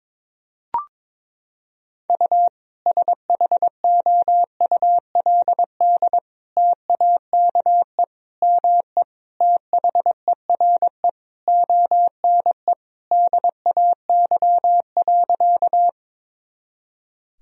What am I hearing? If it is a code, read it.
Morse code, heard as a keyed tone, about 22 wpm, 700 Hz. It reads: U SHOULD TAKE ME THERE ONE DAY.
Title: Sentences from Top 300 Words and Abbreviations 22wpm